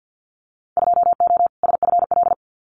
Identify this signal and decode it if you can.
Morse code, heard as a keyed tone, about 50 wpm, 700 Hz. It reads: 2O SFL
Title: Winter Field Day Exchanges 50wpm